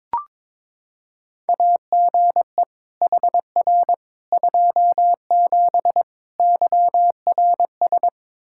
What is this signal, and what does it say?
Morse code, heard as a keyed tone, about 21 wpm, 700 Hz. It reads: AGE HR 27 YRS
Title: Practice QSO - Exchange 3 - More Info - 22wpm